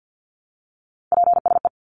Morse code, heard as a keyed tone, about 50 wpm, 700 Hz. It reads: PSE
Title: Abbreviations 50wpm